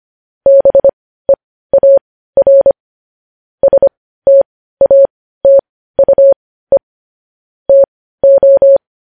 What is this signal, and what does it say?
Morse code, heard as a keyed tone, about 25 wpm, spaced as for 15 wpm, 550 Hz. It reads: BEAR STATUE TO
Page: news-headlines-in-morse-code-at-15-wpm